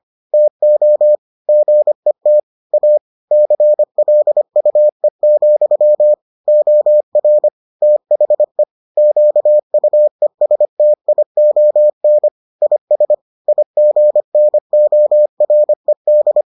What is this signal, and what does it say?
Morse code, heard as a keyed tone, about 25 wpm, 600 Hz. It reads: TO GET A CLUE, OR THE QUESTION IS IGNORED